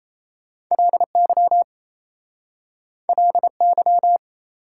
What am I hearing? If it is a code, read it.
Morse code, heard as a keyed tone, about 36 wpm, 690 Hz. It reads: LY LY